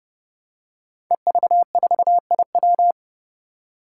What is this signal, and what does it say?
Morse code, heard as a keyed tone, about 30 wpm, 700 Hz. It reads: EV4IW
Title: Five Challenging Letter-Number Combinations 30wpm